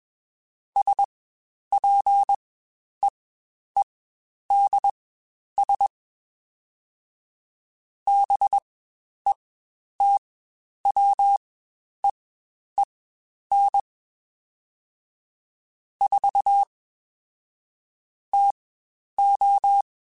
Morse code, heard as a keyed tone, about 21 wpm, spaced as for 10 wpm, 800 Hz. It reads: SPEEDS BETWEEN 4 TO
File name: Spaced Characters / Spaced Characters-mp3s / Weather 4 21 sp @20